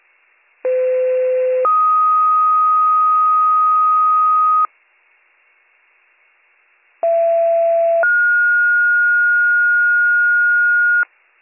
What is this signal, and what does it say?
Two-tone page, heard as a one-second tone, then a three-second tone, about 510 then 1250 Hz, about 670 then 1430 Hz. It is heard from 0.6 to 11.0 s.